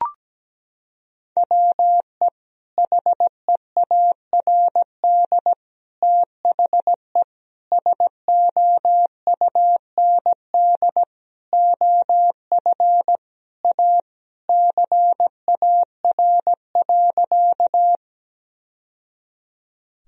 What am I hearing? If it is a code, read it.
Morse code, heard as a keyed tone, about 17 wpm, 700 Hz. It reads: WE HEARD THE SOUND OF A CAR.
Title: Sentences from Top 1000 Words - Review Part 1 - 17wpm